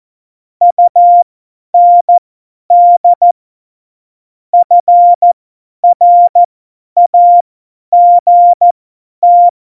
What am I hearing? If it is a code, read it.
Morse code, heard as a keyed tone, about 14 wpm, spaced as for 10 wpm, 700 Hz. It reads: UND FRAGT
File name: Morsekurs / Witze / Witz 15. Dezember 2020, 50 F70